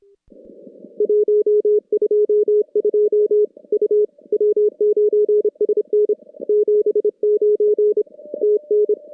Morse code, heard as a keyed tone, about 26 wpm, 410 Hz. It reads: 122 U W9SN 79 TN
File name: contests / CW-SS-k5tr- -2300